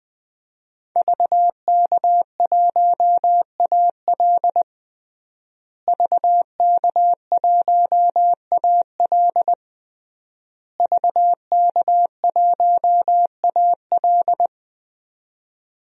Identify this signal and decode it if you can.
Morse code, heard as a keyed tone, about 20 wpm, 700 Hz. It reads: VK1AL VK1AL VK1AL
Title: DXCC-100 Call Signs - 5 Characters - Repeated 3x - 20wpm